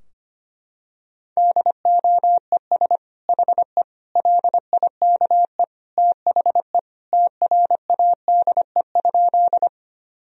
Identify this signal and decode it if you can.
Morse code, heard as a keyed tone, about 25 wpm, 700 Hz. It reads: DOES HE LIKE THE TRADE?
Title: Sentences from Top 1000 Words - Review Part 4 - 25wpm